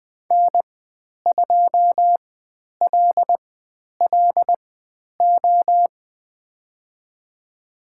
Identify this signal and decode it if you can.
Morse code, heard as a keyed tone, about 20 wpm, spaced as for 10 wpm, 700 Hz. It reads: N2LLO
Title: 095 Lesson 16g - Letter c - Call Signs 20wpm